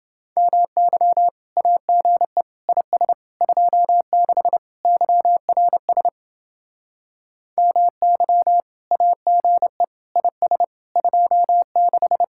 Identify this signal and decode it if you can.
Morse code, heard as a keyed tone, about 28 wpm, 700 Hz. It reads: MY AGE IS 26 YRS MY AGE IS 26